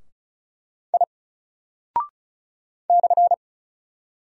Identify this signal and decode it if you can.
Morse code, heard as a keyed tone, about 36 wpm, 700 Hz. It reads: I /
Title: Single Character (Letter:Number:Punctuation:Prosign) 35wpm